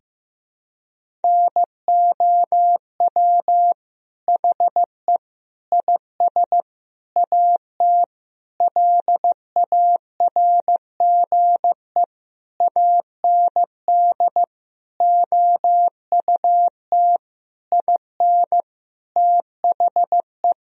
Morse code, heard as a keyed tone, about 15 wpm, 700 Hz. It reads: NOW HE IS AT LARGE AND OUT IN THE
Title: Sentences from Top 1000 Words - Review Part 1 - 15wpm